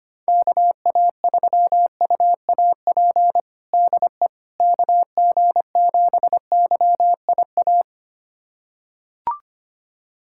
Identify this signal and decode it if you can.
Morse code, heard as a keyed tone, about 25 wpm, 700 Hz. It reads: KA3UAP DE KG7YIA
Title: USA Callsign DE Callsign - 6 Characters - 25wpm